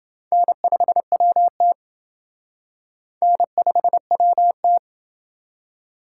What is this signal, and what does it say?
Morse code, heard as a keyed tone, about 27 wpm, 700 Hz. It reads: N5WT N5WT